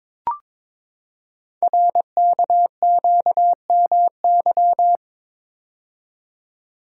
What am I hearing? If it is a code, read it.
Morse code, heard as a keyed tone, about 23 wpm, 700 Hz. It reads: RKQMY
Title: Code Groups - 5 Letter:Numbers - 22wpm